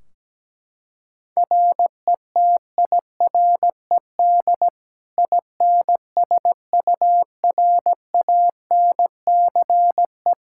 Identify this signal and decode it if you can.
Morse code, heard as a keyed tone, about 17 wpm, 700 Hz. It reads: RETIRED INSURANCE
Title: Practice QSO - Exchange 3 - More Info - 17wpm